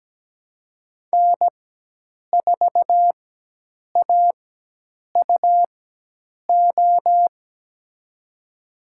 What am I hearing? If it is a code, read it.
Morse code, heard as a keyed tone, about 17 wpm, spaced as for 8 wpm, 700 Hz. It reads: N4AUO
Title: USA Call Signs - 4x Character Spacing - 17wpm